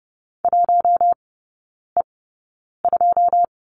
Morse code, heard as a keyed tone, about 30 wpm, spaced as for 9 wpm, 700 Hz. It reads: 1E2
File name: Lesson 19e - Letter w - Three Characters 30wpm